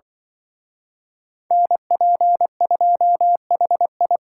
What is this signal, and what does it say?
Morse code, heard as a keyed tone, about 24 wpm, 700 Hz. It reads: NP2HI
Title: Speed Racing - USA 2x2 Call Signs 20wpm